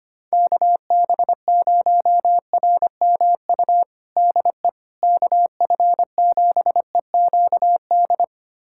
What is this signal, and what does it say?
Morse code, heard as a keyed tone, about 25 wpm, 700 Hz. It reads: KB0RMU DE KF7EQD
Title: USA Callsign DE Callsign - 6 Characters - 25wpm